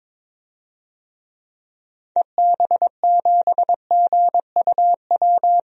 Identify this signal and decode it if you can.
Morse code, heard as a keyed tone, about 21 wpm, 700 Hz. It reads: EB7GUW
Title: DXCC-100 Call Signs - 6 Characters - Repeated 2x - 22wpm